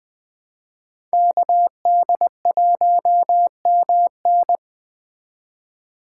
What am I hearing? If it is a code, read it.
Morse code, heard as a keyed tone, about 20 wpm, 700 Hz. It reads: KD1MN
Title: Speed Racing - USA Call Signs 17wpm